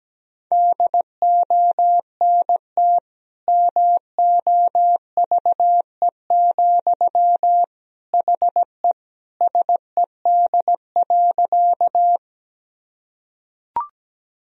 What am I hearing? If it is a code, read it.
Morse code, heard as a keyed tone, about 17 wpm, 700 Hz. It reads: DONT MOVE, HE SED.
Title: Sentences from Top 300 Words and Abbreviations 17wpm